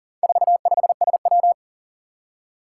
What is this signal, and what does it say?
Morse code, heard as a keyed tone, about 40 wpm, 700 Hz. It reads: 45SW
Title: Four Challenging Letter-Number Combinations 40wpm